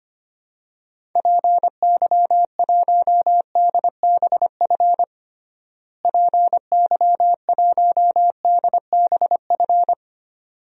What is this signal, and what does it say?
Morse code, heard as a keyed tone, about 25 wpm, 700 Hz. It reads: PY1DBF PY1DBF
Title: DXCC-100 Call Signs - 6 Characters - Repeated 2x - 25wpm